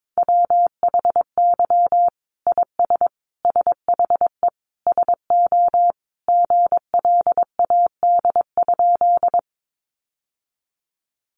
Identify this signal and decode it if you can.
Morse code, heard as a keyed tone, about 22 wpm, 700 Hz. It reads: WHY IS SHE SO GLAD?
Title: Sentences from Top 800 Words 22wpm